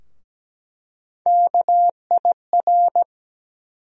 Morse code, heard as a keyed tone, about 17 wpm, 700 Hz. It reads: KIR
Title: USA Call Sign Postfixes - 3 Letters 17wpm